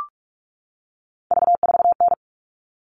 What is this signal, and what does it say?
Morse code, heard as a keyed tone, about 44 wpm, 700 Hz. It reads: V4N
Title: All Cut Numbers - 3-Digit 45wpm